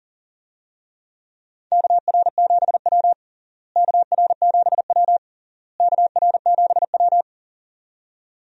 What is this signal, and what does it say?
Morse code, heard as a keyed tone, about 40 wpm, 700 Hz. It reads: KR7W KR7W KR7W
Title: USA Call Signs - 4 Characters - Repeated 3x - 40wpm